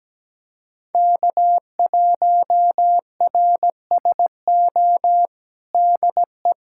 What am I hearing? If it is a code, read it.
Morse code, heard as a keyed tone, about 17 wpm, 700 Hz. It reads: K1RSO DE
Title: USA Callsign DE Callsign - 5 Characters - 17wpm